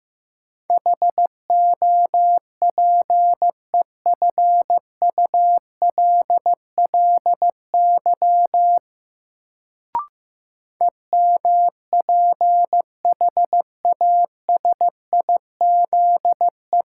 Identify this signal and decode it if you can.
Morse code, heard as a keyed tone, about 15 wpm, 700 Hz. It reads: HOPEFULLY EMPHASIZE
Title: Sentences from Top 1500 Words - New Word Drill Alt - 15wpm